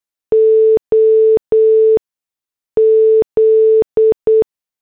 Morse code, heard as a keyed tone, about 8 wpm, 440 Hz. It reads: O Z